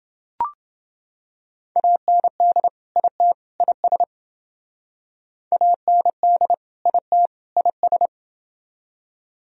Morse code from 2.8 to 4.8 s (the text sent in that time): IT IS